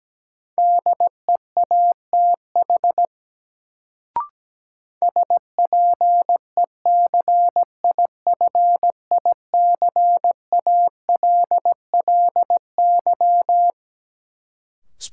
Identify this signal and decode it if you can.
Morse code, heard as a keyed tone, about 17 wpm, 700 Hz. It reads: DEATH SPECIFICALLY
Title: Sentences from Top 1100 Words - New Word Drill Alt - 17wpm